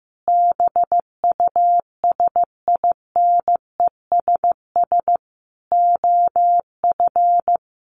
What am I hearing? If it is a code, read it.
Morse code, heard as a keyed tone, about 15 wpm, 700 Hz. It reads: BUSINESS OF
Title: Sentences from Top 1300 Words 15wpm